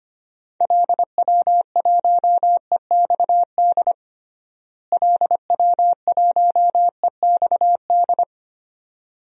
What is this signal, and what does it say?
Morse code, heard as a keyed tone, about 25 wpm, 700 Hz. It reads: LW1EXD LW1EXD
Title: DXCC-100 Call Signs - 6 Characters - Repeated 2x - 25wpm